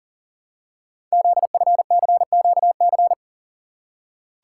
Morse code, heard as a keyed tone, about 42 wpm, 700 Hz. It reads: ZFCQC
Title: Code Groups - 5 Letters - 40wpm